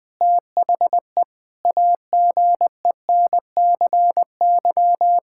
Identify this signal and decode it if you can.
Morse code, heard as a keyed tone, about 21 wpm, 700 Hz. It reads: THE AGENCY
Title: Sentences with 7-Word Phrases - 20wpm